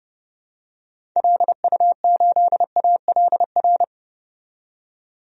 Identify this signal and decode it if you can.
Morse code, heard as a keyed tone, about 31 wpm, 700 Hz. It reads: LU8ALR
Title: DXCC-100 Call Signs - 6 Characters - 30wpm